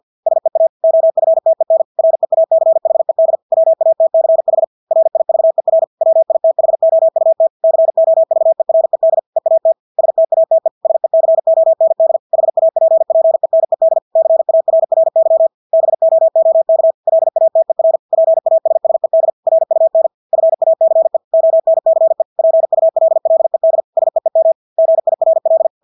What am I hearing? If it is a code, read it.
Morse code, heard as a keyed tone, about 50 wpm, 650 Hz. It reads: SEA OFTEN REACHED WATCH RIVER WITHOUT COVERED EAT STATE SECOND HAPPENED CARRY BOOK LATER PASSED RUN FACE ONCE PULLED SEEM GIRL
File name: Engelske ord 50 wpm